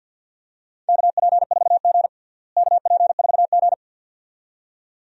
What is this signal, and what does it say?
Morse code, heard as a keyed tone, about 53 wpm, 700 Hz. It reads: KP4G KP4G
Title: DXCC-100 Call Signs - 4 Characters - Repeated 2x - 50wpm